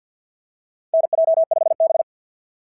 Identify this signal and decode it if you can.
Morse code, heard as a keyed tone, about 48 wpm, 690 Hz. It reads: NJ5B
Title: SOTA QSO Practice 50wpm